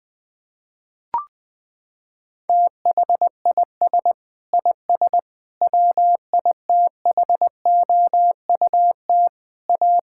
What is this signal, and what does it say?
Morse code, heard as a keyed tone, about 20 wpm, 700 Hz. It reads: THIS IS WITHOUT A
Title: Sentences with 7-Word Phrases - 20wpm